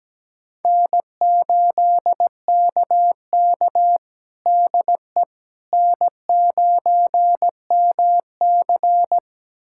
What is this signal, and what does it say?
Morse code, heard as a keyed tone, about 17 wpm, 700 Hz. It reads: N8KK DE N9MC
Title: USA Callsign DE Callsign - 4 Characters - 17wpm